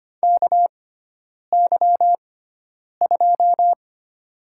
Morse code, heard as a keyed tone, about 25 wpm, spaced as for 9 wpm, 700 Hz. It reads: KY2
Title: USA Call Signs - 6x Character Spacing - 25wpm